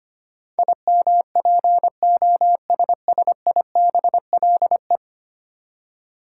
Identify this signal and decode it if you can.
Morse code, heard as a keyed tone, about 25 wpm, 700 Hz. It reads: IMPOSSIBLE
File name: practice / 25wpm / Ten Letter Words - 25wpm